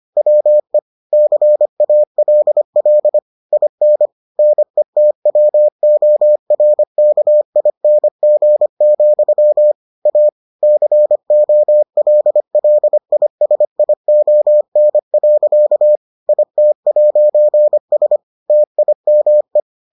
Morse code, heard as a keyed tone, about 25 wpm, 600 Hz. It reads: WE CALL IN NETWORKING, A COLLISION. IT'S TIME